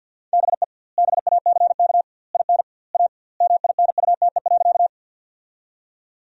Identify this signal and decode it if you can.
Morse code, heard as a keyed tone, about 51 wpm, 700 Hz. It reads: BE BACK IN A MINUTE.
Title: 2x Word-Spacing - Sentences from Top 500 Words - 50wpm